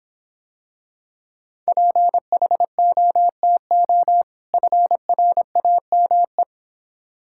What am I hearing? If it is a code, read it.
Morse code, heard as a keyed tone, about 25 wpm, 700 Hz. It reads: PHOTO FRAME